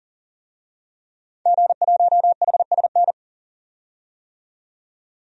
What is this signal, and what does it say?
Morse code, heard as a keyed tone, about 40 wpm, 700 Hz. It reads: G1HSN
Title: DXCC-100 Call Signs - 5 Characters - 40wpm